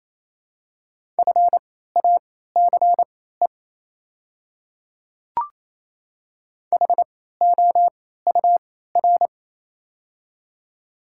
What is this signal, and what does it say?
Morse code, heard as a keyed tone, about 28 wpm, spaced as for 16 wpm, 690 Hz. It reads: FACE HOUR
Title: 3x Character Spacing - Top 500 Words - 28wpm